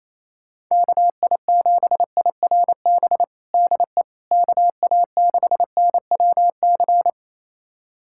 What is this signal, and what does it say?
Morse code, heard as a keyed tone, about 28 wpm, 700 Hz. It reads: KI7IRB DE KA6NWC
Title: USA Callsign DE Callsign - 6 Characters - 28wpm